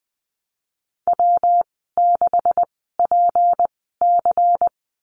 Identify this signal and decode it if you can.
Morse code, heard as a keyed tone, about 20 wpm, spaced as for 15 wpm, 700 Hz. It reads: W6PC
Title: 161 Lesson 26g - Letter g - Call Signs 20wpm